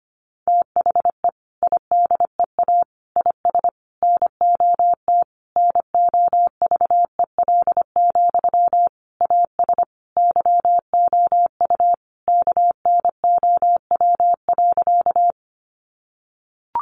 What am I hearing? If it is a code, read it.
Morse code, heard as a keyed tone, about 25 wpm, 700 Hz. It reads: THE IDEA IS NOT NOVEL, AS YOU KNOW.
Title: Sentences from Top 1300 Words 25wpm